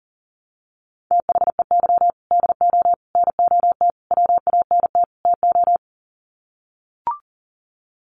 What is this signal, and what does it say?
Morse code, heard as a keyed tone, about 37 wpm, 700 Hz. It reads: THEY DO NOT WANT TO
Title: Sets of 5 Words 40wpm